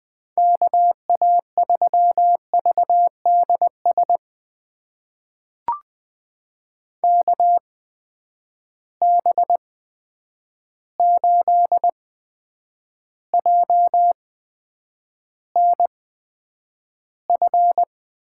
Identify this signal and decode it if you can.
Morse code, heard as a keyed tone, about 20 wpm, spaced as for 9 wpm, 700 Hz. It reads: KA3VDS KB8JNF
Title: USA Call Signs - 6 Characters - 8x Character Spacing - 20wpm